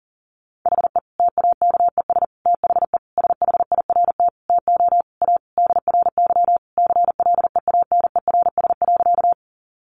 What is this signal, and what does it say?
Morse code, heard as a keyed tone, about 40 wpm, 700 Hz. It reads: HE TAKES THE SHIRT TO A DRY CLEANERS.
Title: Additional Easy Phrases 40wpm